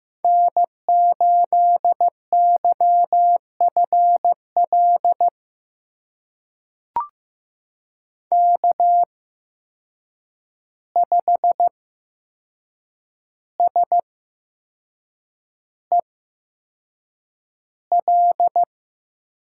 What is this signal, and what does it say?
Morse code, heard as a keyed tone, about 15 wpm, spaced as for 6 wpm, 700 Hz. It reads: N8YFL K5SEL
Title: USA Call Signs - 5 Characters - 8x Character Spacing - 15wpm